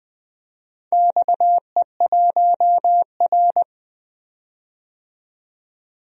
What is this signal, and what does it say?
Morse code, heard as a keyed tone, about 20 wpm, 700 Hz. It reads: XE1R
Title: DXCC-100 Call Signs - 4 Characters - 20wpm